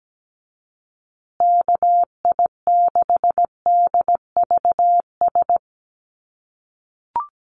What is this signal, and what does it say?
Morse code, heard as a keyed tone, about 17 wpm, 700 Hz. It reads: KI6DVS
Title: USA Call Signs Repeated 2x 17wpm